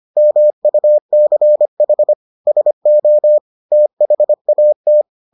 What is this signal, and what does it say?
Morse code, heard as a keyed tone, about 25 wpm, 600 Hz. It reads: MUCH SO THAT